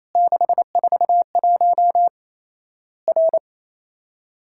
Morse code, heard as a keyed tone, about 28 wpm, 700 Hz. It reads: 641 R
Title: Worked All Europe DX Contest - QTC Exchange - 28wpm